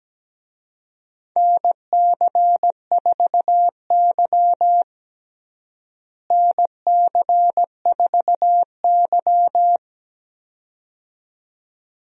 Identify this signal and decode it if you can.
Morse code, heard as a keyed tone, about 17 wpm, 700 Hz. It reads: NC4Y NC4Y
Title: USA Call Signs - 4 Characters - Repeated 2x - 17wpm